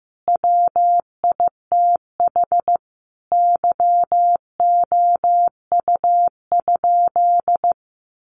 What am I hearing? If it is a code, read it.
Morse code, heard as a keyed tone, about 15 wpm, 700 Hz. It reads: WITH YOU?
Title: Sentences from Top 1500 Words 15wpm